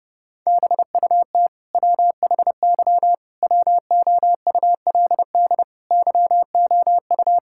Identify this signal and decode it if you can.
Morse code, heard as a keyed tone, about 30 wpm, 700 Hz. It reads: BUT WHY WOULD YOU
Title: Sentences from Top 1000 Words - Review Part 6 - 30wpm